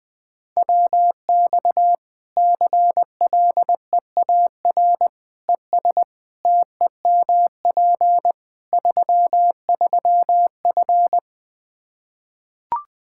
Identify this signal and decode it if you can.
Morse code, heard as a keyed tone, about 20 wpm, 700 Hz. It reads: WX CLEAR ES TEMP 33F
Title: Practice QSO - Exchange 2 - Info - 20wpm